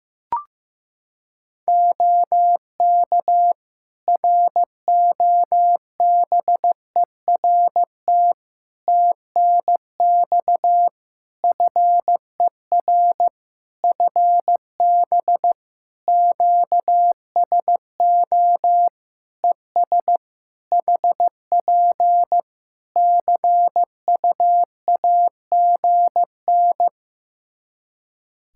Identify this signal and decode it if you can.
Morse code, heard as a keyed tone, about 16 wpm, 700 Hz. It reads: OK ROBERT TNX FER FB QSO ES HP CUAGN
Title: Practice QSO - Exchange 4 - Ending - 15wpm